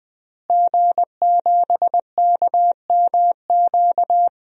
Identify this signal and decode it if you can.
Morse code, heard as a keyed tone, about 21 wpm, 700 Hz. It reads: G7KMQ
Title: DXCC-100 Call Signs - 5 Characters - Repeated 2x - 20wpm